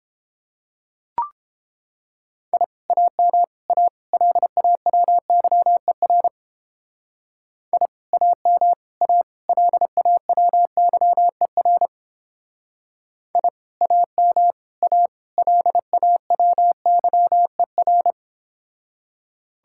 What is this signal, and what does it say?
Morse code, heard as a keyed tone, about 31 wpm, 700 Hz. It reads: I AM A LAWYER I AM A LAWYER I AM A LAWYER